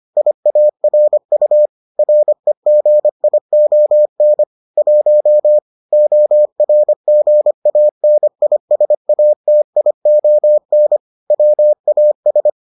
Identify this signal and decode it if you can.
Morse code, heard as a keyed tone, about 25 wpm, 600 Hz. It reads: IARU REGION 1 ORGANISATION WAS